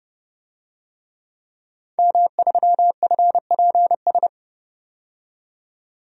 Morse code, heard as a keyed tone, about 31 wpm, 700 Hz. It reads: M3FPS
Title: DXCC-100 Call Signs - 5 Characters - 30wpm